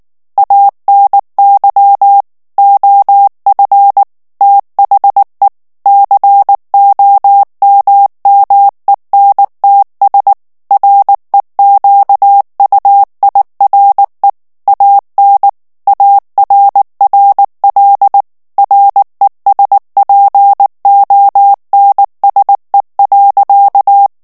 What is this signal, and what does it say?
Morse code, heard as a keyed tone, about 19 wpm, 800 Hz. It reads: ANY OF THE COMMENTS REQUIRE AN ARRL RESPONSE.
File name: ANGLAIS texte clair 3 vitesse 18 mots minute